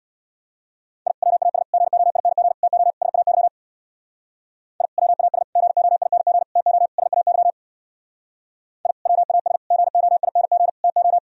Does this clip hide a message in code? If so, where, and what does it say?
Yes: Morse code from 1.1 to 11.2 s, heard as a keyed tone, about 65 wpm, 700 Hz. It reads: I WAS GOING TO SAY I WAS GOING TO SAY I WAS GOING TO